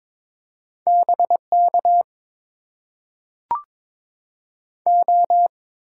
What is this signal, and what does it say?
Morse code, heard as a keyed tone, about 22 wpm, 700 Hz. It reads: B K O
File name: Top 90 QSO Elements - 6x Character Spacing - 22wpm